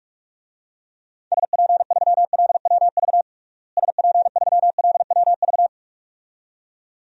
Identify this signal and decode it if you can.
Morse code, heard as a keyed tone, about 41 wpm, 700 Hz. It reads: SP3LWV SP3LWV
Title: DXCC-100 Call Signs - 6 Characters - Repeated 2x - 45wpm